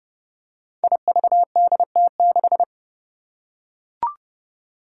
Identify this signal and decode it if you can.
Morse code, heard as a keyed tone, about 30 wpm, 700 Hz. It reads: IVDT6
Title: Five Challenging Letter-Number Combinations 30wpm